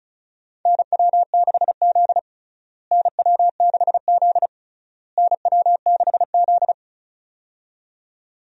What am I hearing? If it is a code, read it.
Morse code, heard as a keyed tone, about 36 wpm, 700 Hz. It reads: NW6Z NW6Z NW6Z
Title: USA Call Signs - 4 Characters - Repeated 3x - 35wpm